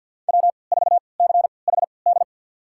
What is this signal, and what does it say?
Morse code, heard as a keyed tone, about 50 wpm, spaced as for 28 wpm, 700 Hz. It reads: W4CHD
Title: USA Call Signs - 5 Characters - 3x Character Spacing - 50wpm